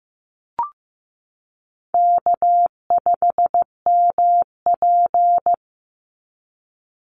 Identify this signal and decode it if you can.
Morse code, heard as a keyed tone, about 15 wpm, 700 Hz. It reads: K5MP
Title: Easy Call Signs Repeated 2x - 15wpm